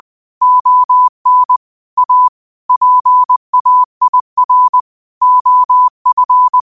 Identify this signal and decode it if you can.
Morse code, heard as a keyed tone, about 20 wpm, 1000 Hz. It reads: ON A PAIR OF